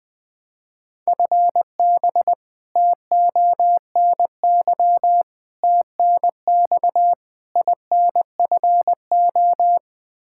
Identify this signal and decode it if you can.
Morse code, heard as a keyed tone, about 20 wpm, 700 Hz. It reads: FB TONY TNX INFO
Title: Practice QSO - Exchange 3 - More Info - 20wpm